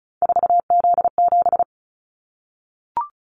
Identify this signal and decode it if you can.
Morse code, heard as a keyed tone, about 36 wpm, 700 Hz. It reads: 4Z7
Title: Callsign Prefixes for Top 50 DXCC - 35wpm